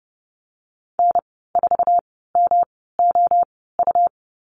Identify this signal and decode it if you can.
Morse code, heard as a keyed tone, about 30 wpm, spaced as for 17 wpm, 700 Hz. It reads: N4MOU
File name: Lesson 19g - Letter w - Call Signs 30wpm